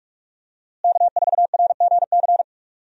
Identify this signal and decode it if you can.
Morse code, heard as a keyed tone, about 46 wpm, 700 Hz. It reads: K4RGC
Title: USA Call Signs - 5 Characters - Repeated 2x - 45wpm